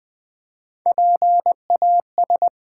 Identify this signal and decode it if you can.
Morse code, heard as a keyed tone, about 20 wpm, 700 Hz. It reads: PAS
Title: Build A Word - 3 Letters At A Time - 6-Letter Words - 20wpm